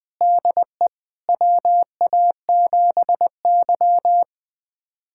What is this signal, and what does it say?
Morse code, heard as a keyed tone, about 20 wpm, 700 Hz. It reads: DE WA7Y
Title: USA Callsign DE Callsign - 4 Characters - 20wpm